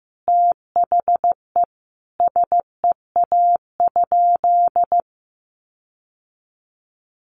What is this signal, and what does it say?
Morse code, heard as a keyed tone, about 15 wpm, 700 Hz. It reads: THE SEA?
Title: Sentences from Top 1500 Words 15wpm